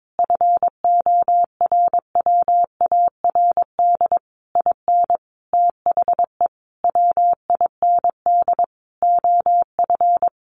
Morse code, heard as a keyed tone, about 22 wpm, 700 Hz. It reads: FORWARD IN THE WIND OF
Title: A Sound Of Thunder 22wpm